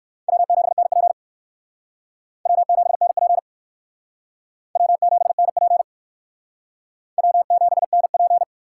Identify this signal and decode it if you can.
Morse code, heard as a keyed tone, about 57 wpm, 700 Hz. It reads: W7NP W7NP W7NP W7NP